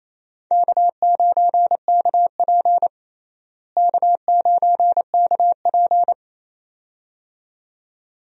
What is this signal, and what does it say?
Morse code, heard as a keyed tone, about 28 wpm, 700 Hz. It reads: K9KP K9KP
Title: USA Call Signs - 4 Characters - Repeated 2x - 28wpm